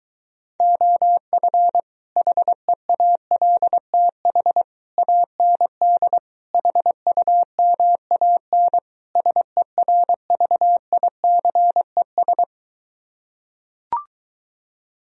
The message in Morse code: OF HEALTH AND HUMAN SERVICES

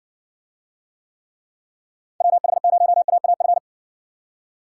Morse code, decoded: WH0AAF

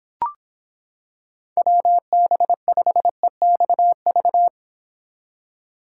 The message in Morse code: WB5EXV

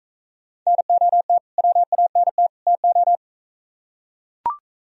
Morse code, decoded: NOT WANT TO

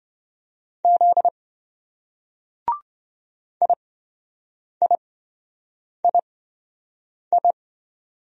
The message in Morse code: Z IIII

